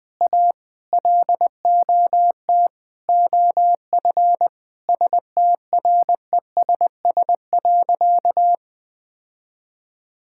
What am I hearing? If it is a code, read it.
Morse code, heard as a keyed tone, about 20 wpm, 700 Hz. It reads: A LOT OF STRESS.
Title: Sentences from Top 1000 Words - Review Part 6 - 20wpm